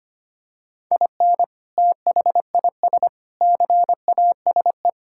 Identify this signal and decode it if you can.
Morse code, heard as a keyed tone, about 25 wpm, 700 Hz. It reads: IN THIS CASE
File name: practice / 25wpm / Sentences with 3-Word Phrases - 25wpm